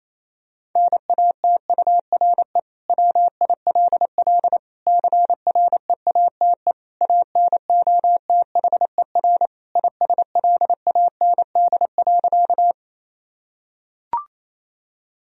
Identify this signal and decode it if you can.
Morse code, heard as a keyed tone, about 27 wpm, 700 Hz. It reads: NATURE WILL CREATE ANOTHER ISLAND.
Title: Sentences from Top 1000 Words - Review Part 3 - 28wpm